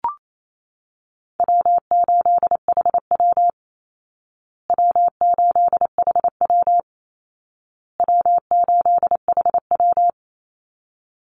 Morse code, heard as a keyed tone, about 28 wpm, 700 Hz. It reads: W8HW W8HW W8HW
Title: Easy Call Signs Repeated 3x - 28wpm